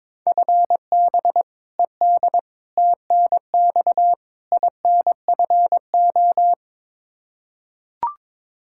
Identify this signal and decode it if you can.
Morse code, heard as a keyed tone, about 22 wpm, 700 Hz. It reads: FB ED TNX INFO
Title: K1USN SST QSO Practice 22wpm